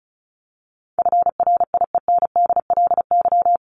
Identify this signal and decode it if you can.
Morse code, heard as a keyed tone, about 34 wpm, 700 Hz. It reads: FRIENDLY
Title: LY Suffix - Top 100 Words with Base Words 35wpm